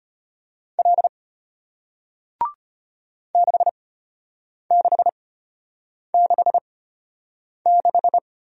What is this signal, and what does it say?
Morse code, heard as a keyed tone, about 33 wpm, spaced as for 8 wpm, 700 Hz. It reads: L 6666